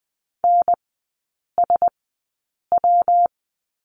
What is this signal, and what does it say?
Morse code, heard as a keyed tone, about 19 wpm, spaced as for 8 wpm, 700 Hz. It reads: NSW
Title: 191 Lesson 31e - Letter v - Three Characters 20wpm